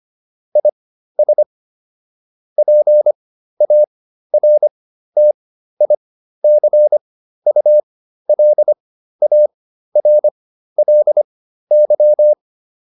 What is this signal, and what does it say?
Morse code, heard as a keyed tone, about 25 wpm, spaced as for 13 wpm, 600 Hz. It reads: IS PARTICULARLY